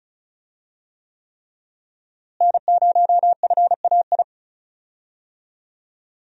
Morse code, heard as a keyed tone, about 36 wpm, 700 Hz. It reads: N0FAI
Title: USA Call Signs - 5 Characters - Repeated 3x - 35wpm